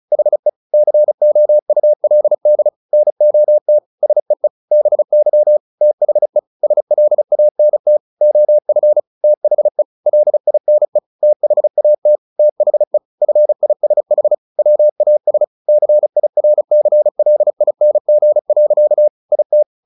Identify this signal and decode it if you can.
Morse code, heard as a keyed tone, about 34 wpm, 600 Hz. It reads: HE COULD NOT SEE BY THE SLANT OF THE LINE THAT THE FISH WAS CIRCLING. IT
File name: the old man and the sea 2 35wpm